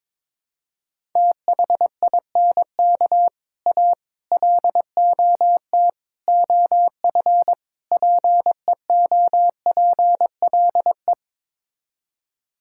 Morse code, heard as a keyed tone, about 22 wpm, 700 Hz. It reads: THINK A LOT OF PEOPLE
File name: practice / 22wpm / Build A Phrase - 5-Word Phrases - 22wpm